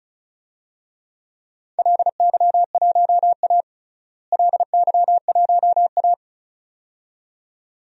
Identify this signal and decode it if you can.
Morse code, heard as a keyed tone, about 34 wpm, 690 Hz. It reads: LY1A LY1A
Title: DXCC-100 Call Signs - 4 Characters - Repeated 2x - 35wpm